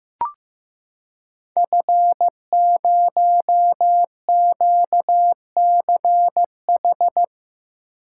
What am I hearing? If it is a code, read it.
Morse code, heard as a keyed tone, about 15 wpm, 700 Hz. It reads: F0QCH
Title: Code Groups - 5 Letter:Numbers - 15wpm